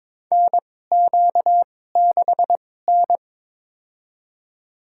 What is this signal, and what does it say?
Morse code, heard as a keyed tone, about 23 wpm, spaced as for 16 wpm, 700 Hz. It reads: NQ6N
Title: USA Call Signs - 4 Characters - 2x Character Spacing - 22wpm